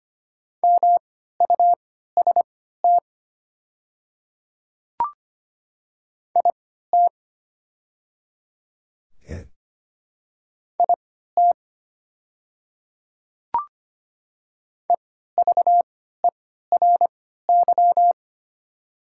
Morse code, heard as a keyed tone, about 25 wpm, spaced as for 14 wpm, 700 Hz. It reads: MUST IT IT EVERY
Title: 3x Character Spacing - Top 200 Words - 25wpm